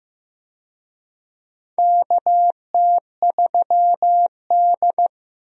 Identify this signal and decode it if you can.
Morse code, heard as a keyed tone, about 15 wpm, 700 Hz. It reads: KT3D
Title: USA Call Signs - 4 Characters - Repeated 2x - 15wpm